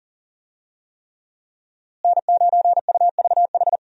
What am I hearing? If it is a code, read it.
Morse code, heard as a keyed tone, about 38 wpm, 700 Hz. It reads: N9UVH
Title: USA Call Signs - 5 Characters - Repeated 3x - 40wpm